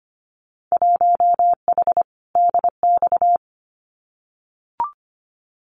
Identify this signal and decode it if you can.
Morse code, heard as a keyed tone, about 25 wpm, 700 Hz. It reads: 1H DX
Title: Winter Field Day Exchanges 25wpm